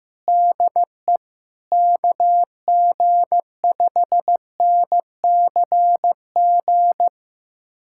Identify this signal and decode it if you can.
Morse code, heard as a keyed tone, about 15 wpm, 700 Hz. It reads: DE KG5NCG
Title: USA Callsign DE Callsign 15wpm